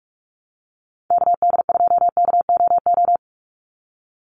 Morse code, heard as a keyed tone, about 44 wpm, 700 Hz. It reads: KD2KOO
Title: USA Call Signs 45wpm